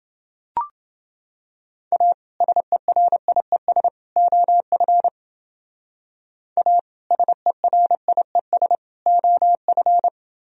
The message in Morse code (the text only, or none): A SERIES OF A SERIES OF